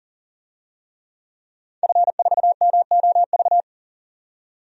Morse code, heard as a keyed tone, about 40 wpm, 700 Hz. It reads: F4MOV
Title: DXCC-100 Call Signs - 5 Characters - 40wpm